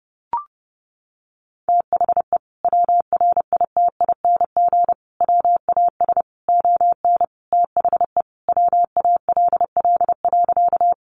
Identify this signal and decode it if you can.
Morse code, heard as a keyed tone, about 30 wpm, 700 Hz. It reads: THE WRITING WAS ON THE WALL.
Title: Sentences from Top 1100 Words 30wpm